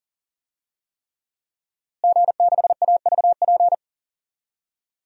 Morse code, heard as a keyed tone, about 40 wpm, 700 Hz. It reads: G6AVP
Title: DXCC-100 Call Signs - 5 Characters - Repeated 3x - 40wpm